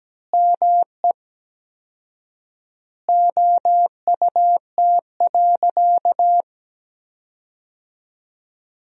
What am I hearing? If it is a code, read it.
Morse code, heard as a keyed tone, about 17 wpm, 700 Hz. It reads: ME OUT.
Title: 4x Word-Spacing - Sentences from Top 500 Words - 17wpm